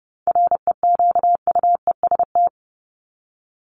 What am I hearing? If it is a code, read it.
Morse code, heard as a keyed tone, about 30 wpm, 700 Hz. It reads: REQUEST
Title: Top 1751-2000 Words 30wpm